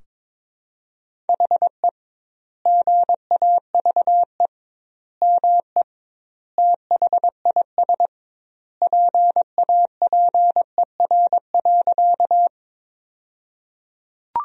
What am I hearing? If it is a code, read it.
Morse code, heard as a keyed tone, about 22 wpm, 700 Hz. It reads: HE GAVE ME THIS PAPER.
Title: 2x Word-Spacing - Sentences from Top 500 Words - 22wpm